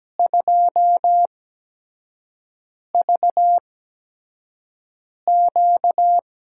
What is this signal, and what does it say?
Morse code, heard as a keyed tone, about 17 wpm, spaced as for 5 wpm, 700 Hz. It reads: 2VQ
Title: USA Call Signs - 8x Character Spacing - 17wpm